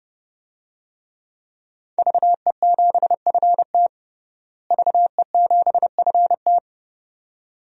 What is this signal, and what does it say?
Morse code, heard as a keyed tone, about 30 wpm, 700 Hz. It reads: VE7FT VE7FT
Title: DXCC-100 Call Signs - 5 Characters - Repeated 2x - 30wpm